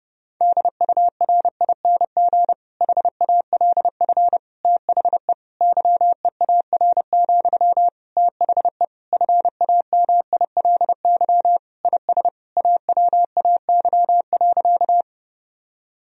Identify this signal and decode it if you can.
Morse code, heard as a keyed tone, about 30 wpm, 700 Hz. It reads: DURING HALF THE YEAR, THE FAMILY IS AWAY.
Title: Sentences from Top 1000 Words - Review Part 2 - 30wpm